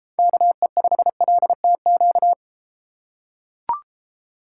Morse code, heard as a keyed tone, about 33 wpm, 700 Hz. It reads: KE5LTQ